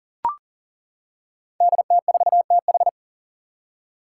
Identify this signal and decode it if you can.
Morse code, heard as a keyed tone, about 40 wpm, 700 Hz. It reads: DT4TH
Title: Five Challenging Letter-Number Combinations 40wpm